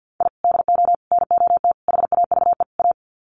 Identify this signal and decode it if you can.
Morse code, heard as a keyed tone, about 50 wpm, 700 Hz. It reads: I DO NOT HAVE A
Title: Sets of 5 Words 50wpm